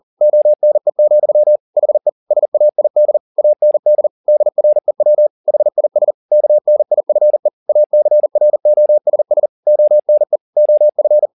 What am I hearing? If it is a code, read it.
Morse code, heard as a keyed tone, about 40 wpm, 600 Hz. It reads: ONE, HE SAID AND DREW HIS KNIFE ACROSS ONE OF